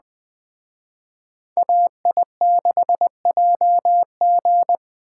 Morse code, heard as a keyed tone, about 20 wpm, 700 Hz. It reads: AI6JG
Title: Speed Racing - USA 2x2 Call Signs 17wpm